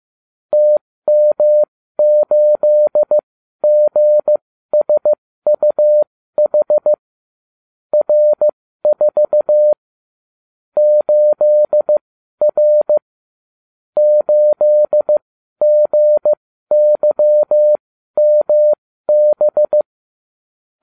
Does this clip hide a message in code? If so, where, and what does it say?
Yes: Morse code from 0.5 to 19.8 s, heard as a keyed tone, about 15 wpm, spaced as for 12 wpm, 600 Hz. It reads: TM8GSUH R4 8R 8GYMB